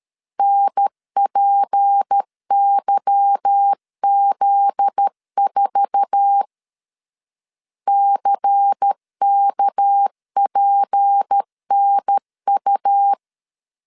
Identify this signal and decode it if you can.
Morse code, heard as a keyed tone, about 13 wpm, 800 Hz. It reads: NPYZ4 CKPNU
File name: CW / DET / 1 9 1500C S A 12wpm 3800